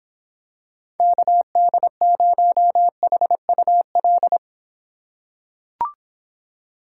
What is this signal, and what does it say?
Morse code, heard as a keyed tone, about 26 wpm, 700 Hz. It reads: KD0HUL